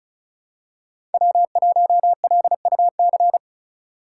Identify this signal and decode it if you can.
Morse code, heard as a keyed tone, about 34 wpm, 700 Hz. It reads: W1LUC
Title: USA Call Signs - 5 Characters - 2x Character Spacing - 35wpm